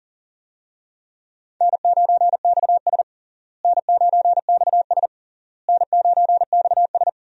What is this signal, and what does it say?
Morse code, heard as a keyed tone, about 40 wpm, 700 Hz. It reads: N9XS N9XS N9XS
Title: USA Call Signs - 4 Characters - Repeated 3x - 40wpm